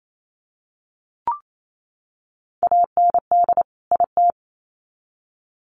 Morse code, heard as a keyed tone, about 28 wpm, 700 Hz. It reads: AND IT
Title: Sets of 2 Words 28wpm